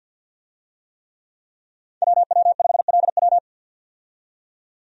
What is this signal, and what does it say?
Morse code, heard as a keyed tone, about 51 wpm, 700 Hz. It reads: WW5LW
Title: USA Call Signs - 5 Characters - 50wpm